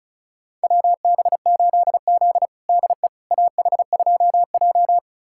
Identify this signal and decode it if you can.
Morse code, heard as a keyed tone, about 36 wpm, 700 Hz. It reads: WB8Z DE AH2J
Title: USA Callsign DE Callsign - 4 Characters - 35wpm